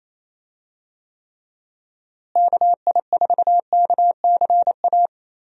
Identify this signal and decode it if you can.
Morse code, heard as a keyed tone, about 28 wpm, 700 Hz. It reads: KI4KCA
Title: USA Call Signs - 6 Characters - Repeated 3x - 28wpm